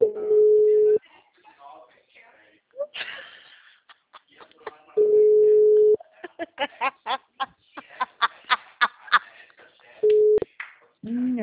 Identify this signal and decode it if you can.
Ringback tone (Europe, 425 Hz), with speech.
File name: out- -2003- - - .84871